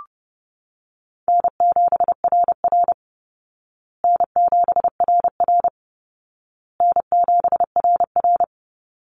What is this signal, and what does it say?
Morse code, heard as a keyed tone, about 30 wpm, 700 Hz. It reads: N7RR N7RR N7RR
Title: Call Signs Repeated 2x - 30wpm